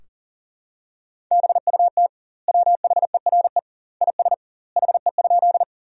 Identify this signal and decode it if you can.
Morse code, heard as a keyed tone, about 40 wpm, 700 Hz. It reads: BUT WHERE IS HE?
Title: 2x Word-Spacing - Sentences from Top 200 Words - 40wpm